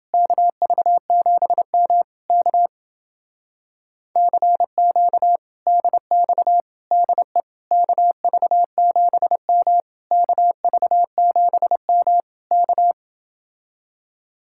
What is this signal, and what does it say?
Morse code, heard as a keyed tone, about 28 wpm, 700 Hz. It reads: KV7M K CQ DX DE KV7M KV7M K